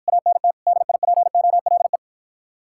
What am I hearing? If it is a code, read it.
Morse code, heard as a keyed tone, about 54 wpm, 700 Hz. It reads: ANT DIPOLE